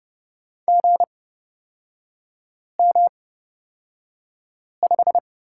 Morse code, heard as a keyed tone, about 30 wpm, spaced as for 5 wpm, 700 Hz. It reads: GM5
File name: Lesson 36e - BT - Three Characters 30wpm